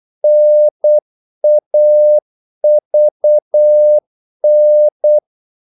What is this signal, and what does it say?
Morse code, heard as a keyed tone, about 8 wpm, 600 Hz. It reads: NAVN
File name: Korte ord DK 8wpm